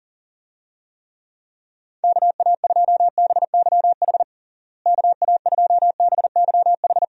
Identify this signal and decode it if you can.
Morse code, heard as a keyed tone, about 40 wpm, 700 Hz. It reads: KA2BYH KA2BYH
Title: USA Call Signs - 6 Characters - Repeated 2x - 40wpm